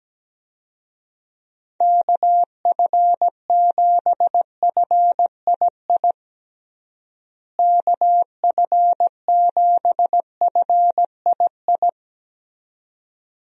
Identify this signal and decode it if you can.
Morse code, heard as a keyed tone, about 17 wpm, 700 Hz. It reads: KF7FII KF7FII
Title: USA Call Signs - 6 Characters - Repeated 2x - 17wpm